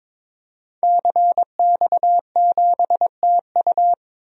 Morse code, heard as a keyed tone, about 22 wpm, 700 Hz. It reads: CX7TU
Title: DXCC-100 Call Signs - 5 Characters - Repeated 3x - 22wpm